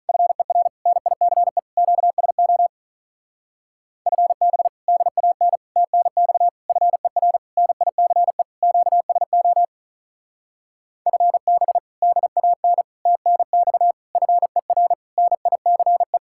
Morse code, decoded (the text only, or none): FER NICE QSO FB DAN TNX FER NICE QSO FB DAN TNX FER NICE